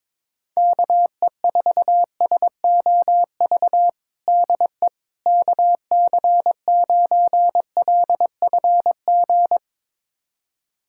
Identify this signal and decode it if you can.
Morse code, heard as a keyed tone, about 22 wpm, 700 Hz. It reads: KE4SOV DE KC9LFG
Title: USA Callsign DE Callsign - 6 Characters - 22wpm